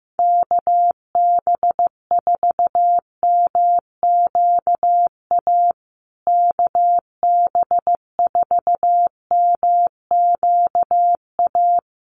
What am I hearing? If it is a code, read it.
Morse code, heard as a keyed tone, about 15 wpm, 700 Hz. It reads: KB4MQA KB4MQA
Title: USA Call Signs Repeated 3x 15wpm